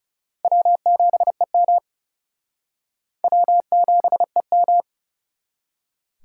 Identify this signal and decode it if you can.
Morse code, heard as a keyed tone, about 37 wpm, 700 Hz. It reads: W7EM W7EM
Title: Speed Racing - USA 1x2 Call Signs 30wpm